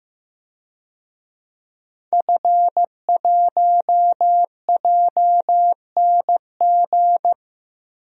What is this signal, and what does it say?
Morse code, heard as a keyed tone, about 15 wpm, 700 Hz. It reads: F1JNG
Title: DXCC-100 Call Signs - 5 Characters - 15wpm